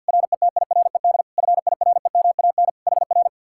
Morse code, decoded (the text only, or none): RETIRED FIREMAN HR